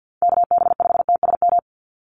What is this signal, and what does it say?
Morse code, heard as a keyed tone, about 49 wpm, 700 Hz. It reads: KB5TSM
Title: USA Call Signs 50wpm